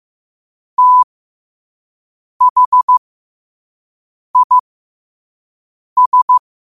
Morse code, heard as a keyed tone, about 15 wpm, spaced as for 5 wpm, 1000 Hz. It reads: THIS
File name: qotdfiles / 5wpm